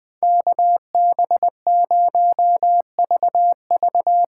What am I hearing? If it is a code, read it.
Morse code, heard as a keyed tone, about 20 wpm, 700 Hz. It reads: KB0VV
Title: USA Call Signs - 6 Characters - 6x Character Spacing - 20wpm